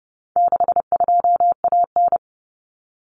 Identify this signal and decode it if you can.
Morse code, heard as a keyed tone, about 30 wpm, 700 Hz. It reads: 62AN
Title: Common Cut Numbers - 4-Digit 30wpm